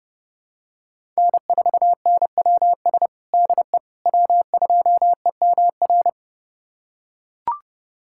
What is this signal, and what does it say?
Morse code, heard as a keyed tone, about 30 wpm, 700 Hz. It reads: N4NWS DE W2EMR
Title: USA Callsign DE Callsign - 5 Characters - 30wpm